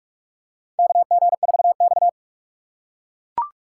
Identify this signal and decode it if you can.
Morse code, heard as a keyed tone, about 44 wpm, 700 Hz. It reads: KG4X
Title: USA Call Signs - 6x Character Spacing - 45wpm